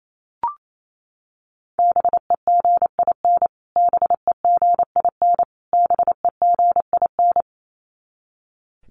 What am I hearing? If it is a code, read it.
Morse code, heard as a keyed tone, about 28 wpm, 700 Hz. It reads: BEGIN BEGIN BEGIN
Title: Top 500 Words Repeated 28wpm